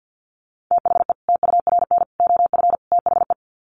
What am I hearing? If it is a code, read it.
Morse code, heard as a keyed tone, about 50 wpm, 700 Hz. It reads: THE TURN OF THE
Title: Sets of 4 Words 50wpm